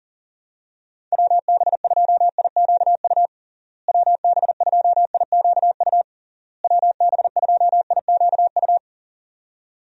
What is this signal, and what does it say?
Morse code, heard as a keyed tone, about 40 wpm, 700 Hz. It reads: WB2IQU WB2IQU WB2IQU
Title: USA Call Signs - 6 Characters - Repeated 3x - 40wpm